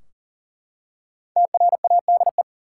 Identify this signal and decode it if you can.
Morse code, heard as a keyed tone, about 40 wpm, 700 Hz. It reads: TRADE
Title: Five Letter Words - 40wpm